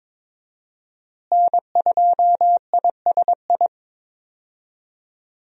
Morse code, heard as a keyed tone, about 21 wpm, 700 Hz. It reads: N2ISI
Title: USA Call Signs - 5 Characters - Repeated 3x - 22wpm